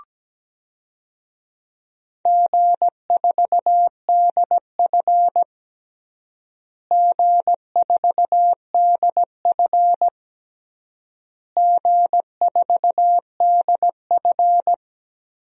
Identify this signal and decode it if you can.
Morse code, heard as a keyed tone, about 17 wpm, 700 Hz. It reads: G4DF G4DF G4DF
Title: DXCC-100 Call Signs - 4 Characters - Repeated 3x - 17wpm